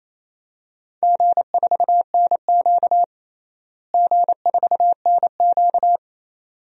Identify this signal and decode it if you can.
Morse code, heard as a keyed tone, about 28 wpm, 700 Hz. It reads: G4NQ G4NQ
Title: DXCC-100 Call Signs - 4 Characters - Repeated 2x - 28wpm